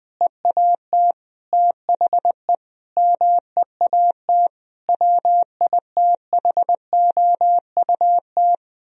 Morse code, heard as a keyed tone, about 20 wpm, 700 Hz. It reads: EAT THE MEAT WITHOUT
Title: Sentences from Top 1000 Words - Review Part 3 - 20wpm